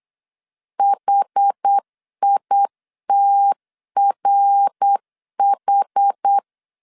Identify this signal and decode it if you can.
Morse code, heard as a keyed tone, about 8 wpm, 800 Hz. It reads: HITRH
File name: CW / DET / 1 3 1500C S B 8wpm 4141